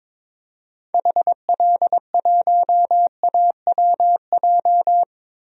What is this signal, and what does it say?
Morse code, heard as a keyed tone, about 22 wpm, 700 Hz. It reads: HL1AWJ
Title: DXCC-100 Call Signs - 6 Characters - Repeated 2x - 22wpm